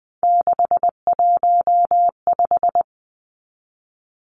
Morse code, heard as a keyed tone, about 20 wpm, 700 Hz. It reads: 615
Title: Numbers - 3-Digit 20wpm